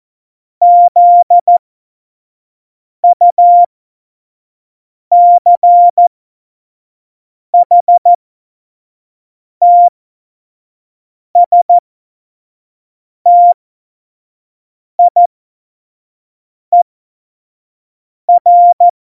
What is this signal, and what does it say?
Morse code, heard as a keyed tone, about 14 wpm, spaced as for 5 wpm, 700 Hz. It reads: ZUCHTSTIER
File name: Morsekurs / Witze / Witz 27. Oktober 2020, 25 F70